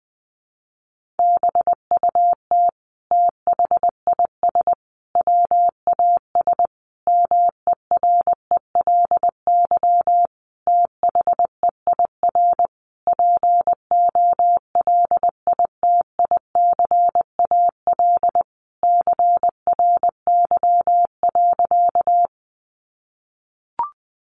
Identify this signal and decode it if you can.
Morse code, heard as a keyed tone, about 20 wpm, 700 Hz. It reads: BUT THIS WAS MERELY THEIR POLITICAL CRY.
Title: Sentences from Top 1400 Words 20wpm